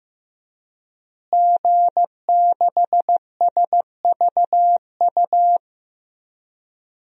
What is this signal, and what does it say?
Morse code, heard as a keyed tone, about 15 wpm, 700 Hz. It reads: G6SVU
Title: DXCC-100 Call Signs - 5 Characters - Repeated 3x - 15wpm